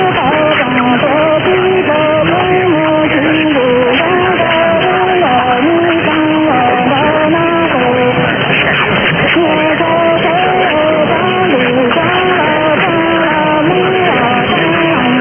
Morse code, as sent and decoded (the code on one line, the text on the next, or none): none